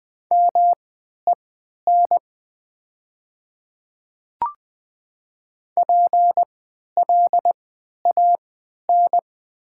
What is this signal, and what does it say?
Morse code, heard as a keyed tone, about 20 wpm, spaced as for 11 wpm, 700 Hz. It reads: MEN PLAN
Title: 3x Character Spacing - Top 500 Words - 20wpm